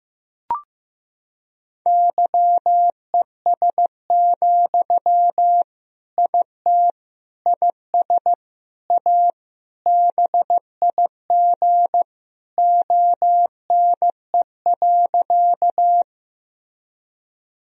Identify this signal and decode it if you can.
Morse code, heard as a keyed tone, about 15 wpm, 700 Hz. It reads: YES, IT IS A BIG ONE.
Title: Sentences from Top 1000 Words - Review Part 1 - 15wpm